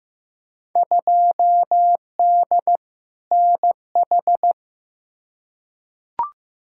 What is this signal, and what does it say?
Morse code, heard as a keyed tone, about 15 wpm, 700 Hz. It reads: 2D NH
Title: Field Day Exchanges Encore - 15wpm